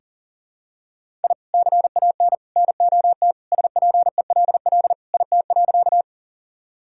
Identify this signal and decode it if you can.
Morse code, heard as a keyed tone, about 40 wpm, 700 Hz. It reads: I CAN NOT SPELL IT.
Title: Sentences from Top 1000 Words - Review Part 1 - 40wpm